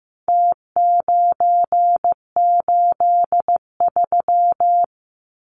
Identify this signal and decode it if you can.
Morse code, heard as a keyed tone, about 15 wpm, 700 Hz. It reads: T983
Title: Numbers - Contest Style - 4-Digit 15wpm